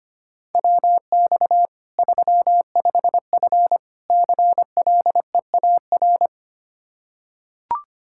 Morse code, heard as a keyed tone, about 25 wpm, 700 Hz. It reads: WX 35F CLEAR
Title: K1USN SST QSO Practice 25wpm